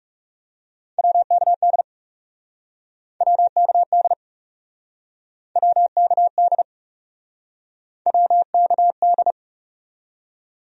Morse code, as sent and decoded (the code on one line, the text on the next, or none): .-- -.- -.. / .-- -.- -.. / .-- -.- -.. / .-- -.- -..
WKD WKD WKD WKD